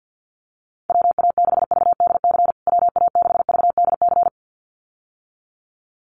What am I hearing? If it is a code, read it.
Morse code, heard as a keyed tone, about 50 wpm, 700 Hz. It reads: WA6VDC WA6VDC
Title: USA Call Signs Repeated 2x 50wpm